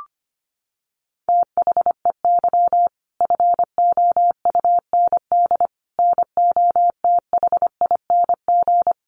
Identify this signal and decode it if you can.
Morse code, heard as a keyed tone, about 25 wpm, 700 Hz. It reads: THEY FOUND NOTHING
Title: Sentences from Top 1500 Words 25wpm